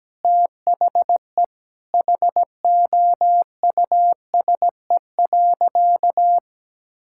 Morse code, heard as a keyed tone, about 17 wpm, 700 Hz. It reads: THE HOUSE.
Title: Sentences from Top 1000 Words - Review Part 6 - 17wpm